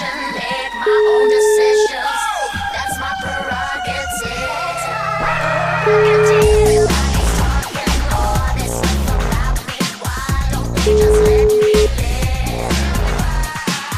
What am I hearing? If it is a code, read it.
Ringback tone (Europe, 425 Hz), with noise.